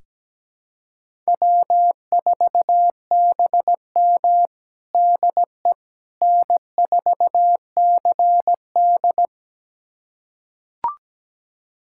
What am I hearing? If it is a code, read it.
Morse code, heard as a keyed tone, about 17 wpm, 700 Hz. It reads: W4BM DE N4CD
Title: USA Callsign DE Callsign - 4 Characters - 17wpm